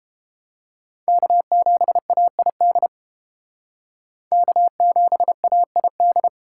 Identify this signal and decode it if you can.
Morse code, heard as a keyed tone, about 30 wpm, 700 Hz. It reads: K7AID K7AID